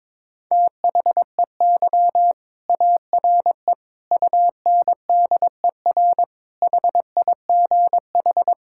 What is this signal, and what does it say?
Morse code, heard as a keyed tone, about 22 wpm, 700 Hz. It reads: THEY ARE UNDER HIGH
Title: Sentences from Top 1600 Words 22wpm